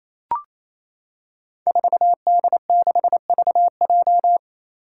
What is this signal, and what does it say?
Morse code, heard as a keyed tone, about 28 wpm, 700 Hz. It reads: 4D6VJ
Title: Five Challenging Letter-Number Combinations 28wpm